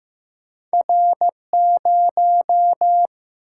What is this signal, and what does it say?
Morse code, heard as a keyed tone, about 15 wpm, 700 Hz. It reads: R0
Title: Single-Letter Single-Number 15wpm